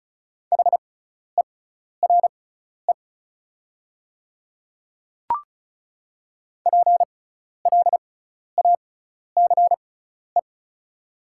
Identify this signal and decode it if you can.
Morse code, heard as a keyed tone, about 36 wpm, spaced as for 12 wpm, 690 Hz. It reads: HERE PLACE
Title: 6x Character Spacing - Top 300 Words - 35wpm